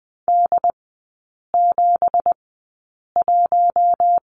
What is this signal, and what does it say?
Morse code, heard as a keyed tone, about 20 wpm, spaced as for 8 wpm, 690 Hz. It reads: D71
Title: 166 Lesson 27e - Number 7 - Three Characters 20wpm